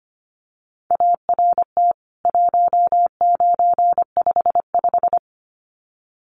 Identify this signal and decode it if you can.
Morse code, heard as a keyed tone, about 25 wpm, 700 Hz. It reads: ART 1955
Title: CWT Exchange 25wpm